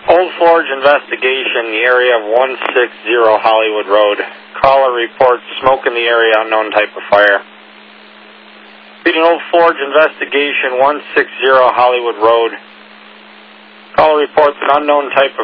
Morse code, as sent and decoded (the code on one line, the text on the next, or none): none